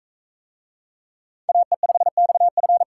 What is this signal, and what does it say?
Morse code, heard as a keyed tone, about 38 wpm, 700 Hz. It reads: AE5XF